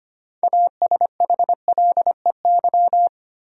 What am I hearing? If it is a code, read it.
Morse code, heard as a keyed tone, about 25 wpm, 700 Hz. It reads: ASHLEY
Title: USA Common Names - 25wpm